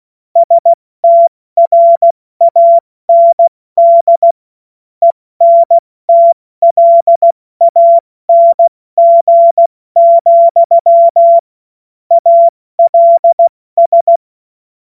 Morse code, heard as a keyed tone, about 16 wpm, 680 Hz. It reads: STRAND ENTLANG, ALS
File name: Morsekurs / Witze / Witz 08. November 2022 70 F80